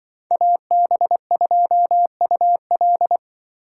Morse code, heard as a keyed tone, about 25 wpm, 700 Hz. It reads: AB2UL
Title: Speed Racing - USA 2x2 Call Signs 20wpm